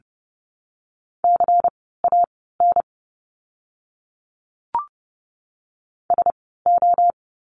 Morse code, heard as a keyed tone, about 30 wpm, spaced as for 17 wpm, 700 Hz. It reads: CAN SO
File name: course / 30:10wpm / 230 Lesson 37f - Letter x - Words 30wpm